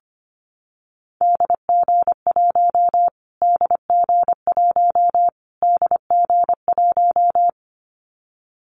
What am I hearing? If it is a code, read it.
Morse code, heard as a keyed tone, about 25 wpm, 700 Hz. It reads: DG1 DG1 DG1
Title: Callsign Prefixes for Top 25 DXCC - Repeated 3x - 25wpm